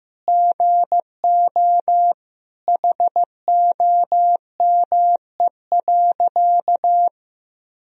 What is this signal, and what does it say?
Morse code, heard as a keyed tone, about 15 wpm, 700 Hz. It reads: GO HOME.
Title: Sentences from Top 200 Words and Abbreviations 15wpm